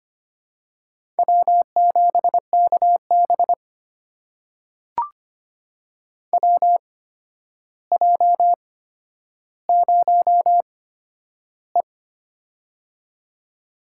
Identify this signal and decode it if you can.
Morse code, heard as a keyed tone, about 25 wpm, spaced as for 11 wpm, 700 Hz. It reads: W7KB WJ0E
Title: USA Call Signs - 4 Characters - 8x Character Spacing - 25wpm